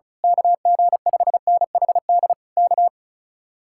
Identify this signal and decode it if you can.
Morse code, heard as a keyed tone, about 33 wpm, 700 Hz. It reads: KG5NHD K
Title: Speed-Racing Common QSO Phrases - 35wpm